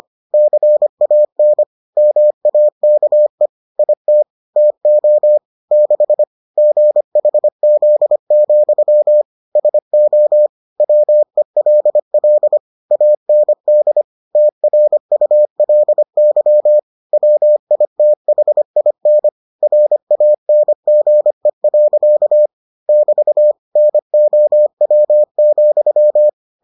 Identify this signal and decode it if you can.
Morse code, heard as a keyed tone, about 25 wpm, 600 Hz. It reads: CAN MAKE IT TO 6 GHZ, SO WELL AND TRULY WITHIN RANGE. <BT> NOW,